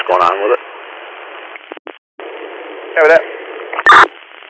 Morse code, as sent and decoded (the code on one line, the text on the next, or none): none